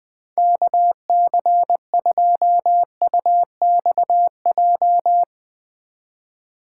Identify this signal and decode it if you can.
Morse code, heard as a keyed tone, about 20 wpm, 700 Hz. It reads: KC2UXJ
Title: Speed Racing - USA Call Signs 20wpm